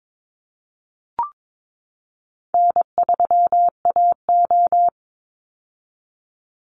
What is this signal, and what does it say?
Morse code, heard as a keyed tone, about 22 wpm, 700 Hz. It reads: N3AO
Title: Call Signs - 22wpm